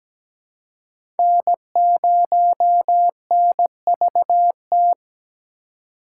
Morse code, heard as a keyed tone, about 17 wpm, 700 Hz. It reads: N0NVT
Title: USA Call Signs - 5 Characters - 2x Character Spacing - 17wpm